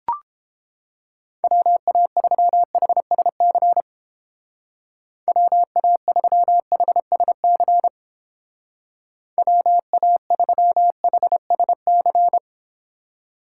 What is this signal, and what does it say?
Morse code, heard as a keyed tone, about 30 wpm, 700 Hz. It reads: WA3HSC WA3HSC WA3HSC